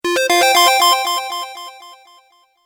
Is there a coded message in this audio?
No.